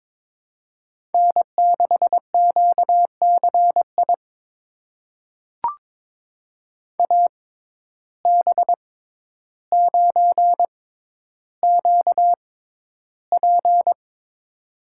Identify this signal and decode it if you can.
Morse code, heard as a keyed tone, about 22 wpm, spaced as for 11 wpm, 700 Hz. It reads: N6QCI AB9QP
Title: USA Call Signs - 5 Characters - 6x Character Spacing - 22wpm